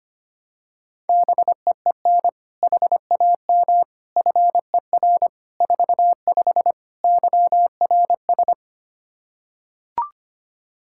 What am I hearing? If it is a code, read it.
Morse code, heard as a keyed tone, about 25 wpm, 700 Hz. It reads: BEEN HAM FER 45 YRS
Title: Practice QSO - Exchange 3 - More Info - 25wpm